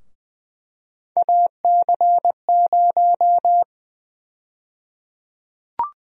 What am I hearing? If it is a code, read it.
Morse code, heard as a keyed tone, about 20 wpm, 700 Hz. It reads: AC0
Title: USA Call Sign Prefixes - Repeated 2x - 20wpm